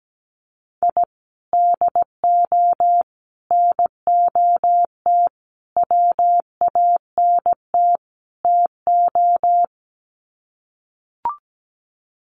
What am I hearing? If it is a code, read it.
Morse code, heard as a keyed tone, about 17 wpm, 700 Hz. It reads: I DO NOT WANT TO
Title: Sets of 5 Words 17wpm